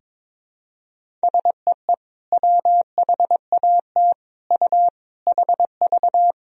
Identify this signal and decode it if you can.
Morse code, heard as a keyed tone, about 22 wpm, 700 Hz. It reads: SEE WHAT U HV
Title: Sentences from Top 500 Words and Abbreviations 22wpm